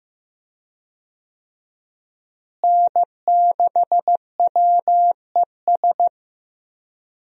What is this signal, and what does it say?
Morse code, heard as a keyed tone, about 15 wpm, 700 Hz. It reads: N6WES
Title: USA Call Signs - 5 Characters - Repeated 3x - 15wpm